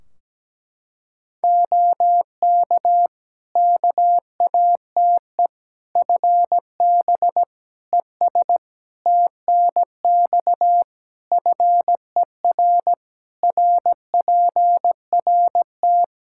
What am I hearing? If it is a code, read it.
Morse code, heard as a keyed tone, about 17 wpm, 700 Hz. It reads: OK KATE FB ES TNX FER RPRT
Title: Practice QSO - Exchange 2 - Info - 17wpm